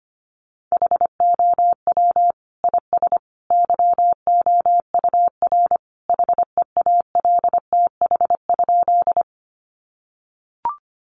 Morse code, heard as a keyed tone, about 25 wpm, 700 Hz. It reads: HOW IS YOUR HEALTH?
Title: Sentences from Top 600 Words 25wpm